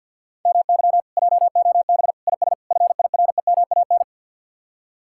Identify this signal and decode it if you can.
Morse code, heard as a keyed tone, about 66 wpm, spaced as for 58 wpm, 700 Hz. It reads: MY JOB IS FIREMAN
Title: Speed-Racing Common QSO Phrases Encore - 50wpm